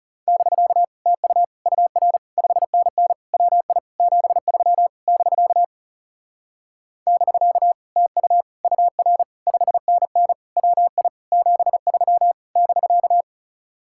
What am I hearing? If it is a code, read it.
Morse code, heard as a keyed tone, about 40 wpm, 700 Hz. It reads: <BK> TU UR 5NN WI 73 <BK> <BK> TU UR 5NN WI 73 <BK>